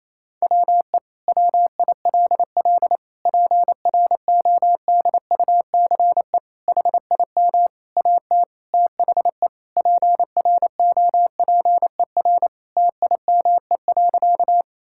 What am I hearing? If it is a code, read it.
Morse code, heard as a keyed tone, about 28 wpm, 700 Hz. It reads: WE WILL PRODUCE HIM AT THE PROPER TIME.
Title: Sentences from Top 1000 Words - Review Part 5 - 28wpm